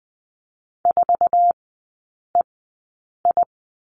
Morse code, heard as a keyed tone, about 20 wpm, spaced as for 8 wpm, 700 Hz. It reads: 4EI
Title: 053 Lesson 10e - Letter r - Three Characters 20wpm